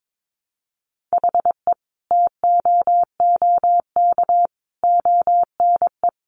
Morse code, heard as a keyed tone, about 22 wpm, 700 Hz. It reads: HE TOOK ONE
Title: Sentences from Top 1300 Words 22wpm